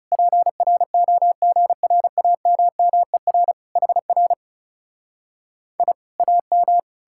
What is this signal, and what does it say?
Morse code, heard as a keyed tone, about 40 wpm, 700 Hz. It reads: PROGRAMMER HR I AM